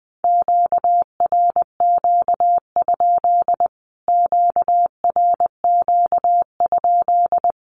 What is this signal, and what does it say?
Morse code, heard as a keyed tone, about 20 wpm, 700 Hz. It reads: QRQ? QRQ?
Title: Q-Codes Repeated 3x 20wpm